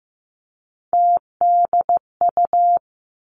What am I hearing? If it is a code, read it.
Morse code, heard as a keyed tone, about 15 wpm, 700 Hz. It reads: TDU
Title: Three Challenging Letter-Number Combinations 15wpm